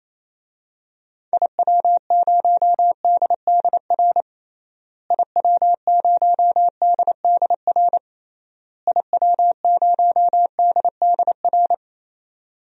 Morse code, heard as a keyed tone, about 28 wpm, 700 Hz. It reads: IW0DDR IW0DDR IW0DDR
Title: DXCC-100 Call Signs - 6 Characters - Repeated 3x - 28wpm